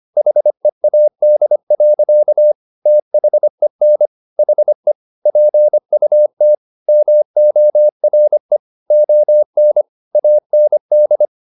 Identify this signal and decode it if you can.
Morse code, heard as a keyed tone, about 25 wpm, 600 Hz. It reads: HEAD. THEN HE PUT MORE ON AND